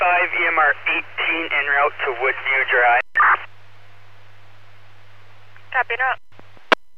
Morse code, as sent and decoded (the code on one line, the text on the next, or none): none